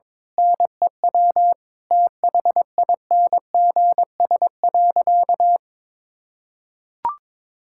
Morse code, heard as a keyed tone, about 22 wpm, 700 Hz. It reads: NEW THINGS.
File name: Difficult Sentences with Hints - 22wpm